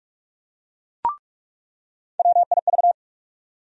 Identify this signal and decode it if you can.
Morse code, heard as a keyed tone, about 43 wpm, 700 Hz. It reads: WIV
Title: USA Call Sign Postfixes - 3 Letters 45wpm